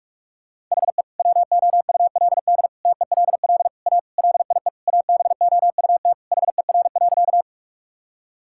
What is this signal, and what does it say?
Morse code, heard as a keyed tone, about 46 wpm, 700 Hz. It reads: HE WOULD TELL A LIE ABOUT HER.
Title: Sentences from Top 1600 Words 45wpm